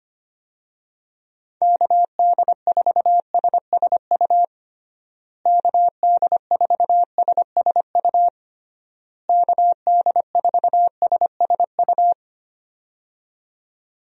Morse code, heard as a keyed tone, about 25 wpm, 700 Hz. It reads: KD4SSU KD4SSU KD4SSU
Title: USA Call Signs - 6 Characters - Repeated 3x - 25wpm